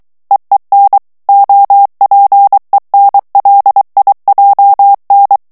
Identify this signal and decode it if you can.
Morse code, heard as a keyed tone, about 23 wpm, 800 Hz. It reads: EEN OPENLIJN
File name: HOLLANDAIS texte clair 1 vitesse 22 mots minute